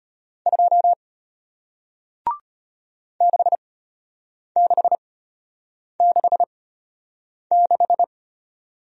Morse code, heard as a keyed tone, about 33 wpm, spaced as for 8 wpm, 700 Hz. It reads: 2 6666